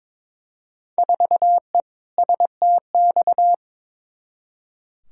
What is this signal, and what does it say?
Morse code, heard as a keyed tone, about 21 wpm, 700 Hz. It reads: 4E STX
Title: Field Day Exchanges Encore - 22wpm